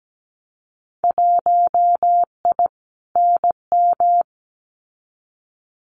Morse code, heard as a keyed tone, about 17 wpm, 700 Hz. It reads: 1I NM
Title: Winter Field Day Exchanges Encore 17wpm